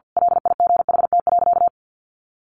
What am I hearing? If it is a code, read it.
Morse code, heard as a keyed tone, about 50 wpm, 700 Hz. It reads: LIGHT.
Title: A Sound Of Thunder 50wpm